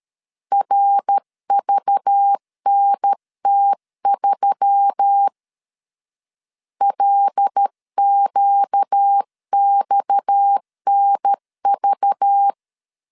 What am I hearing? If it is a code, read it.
Morse code, heard as a keyed tone, about 13 wpm, 800 Hz. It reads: RVNT3 LQXNV
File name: LEZIONI / 7 GF9Z / 7 G 12wpm 2055